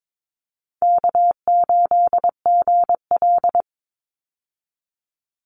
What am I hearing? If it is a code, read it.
Morse code, heard as a keyed tone, about 22 wpm, 700 Hz. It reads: K8GL
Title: Easy Call Signs Repeated 2x - 22wpm